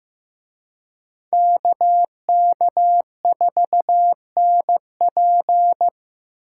Morse code, heard as a keyed tone, about 18 wpm, 700 Hz. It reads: KK4NP
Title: Speed Racing - USA Call Signs 15wpm